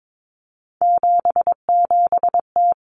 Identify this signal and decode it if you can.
Morse code, heard as a keyed tone, about 22 wpm, 700 Hz. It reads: 77T
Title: Common Cut Numbers - 3-Digit 22wpm